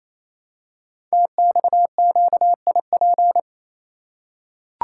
Morse code, heard as a keyed tone, about 28 wpm, 700 Hz. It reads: TXQIP
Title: Code Groups - 5 Letters - 28wpm